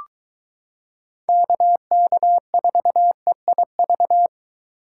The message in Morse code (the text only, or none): KK4EIV